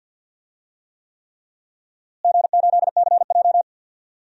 Morse code, heard as a keyed tone, about 51 wpm, 700 Hz. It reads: G8CJ
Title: DXCC-100 Call Signs - 4 Characters - Repeated 3x - 50wpm